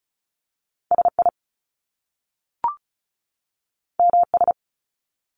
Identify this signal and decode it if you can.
Morse code, heard as a keyed tone, about 36 wpm, 700 Hz. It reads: SI MS
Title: Top Two Letter Combinations in Words 35wpm